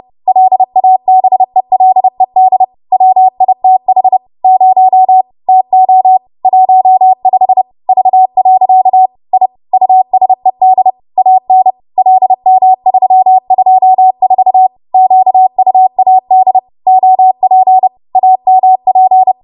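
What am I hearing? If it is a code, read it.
Morse code, heard as a keyed tone, about 30 wpm, 750 Hz. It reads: LABELED WITH 0 TO 15 V. I USED AN LM324 QUAD OP AMP
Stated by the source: W1AW Web Code Practice Files, 30 WPM run from ARRL; text from April 2023 QST.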